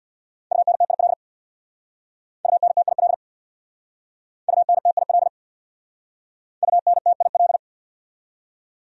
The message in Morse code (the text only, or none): UNTIL UNTIL UNTIL UNTIL